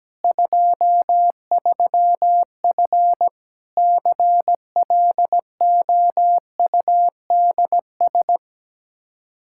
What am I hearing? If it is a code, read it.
Morse code, heard as a keyed tone, about 17 wpm, 700 Hz. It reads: 23F CLOUDS
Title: Practice QSO - Exchange 2 - Info - 17wpm